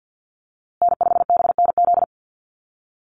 Morse code, heard as a keyed tone, about 48 wpm, 700 Hz. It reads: N5BNZ
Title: USA Call Signs Repeated 3x 50wpm